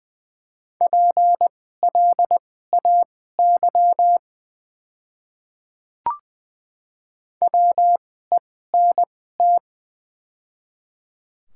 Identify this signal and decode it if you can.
Morse code, heard as a keyed tone, about 20 wpm, spaced as for 15 wpm, 700 Hz. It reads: PLAY WENT
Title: 2x Character Spacing - Top 200 Words - 20wpm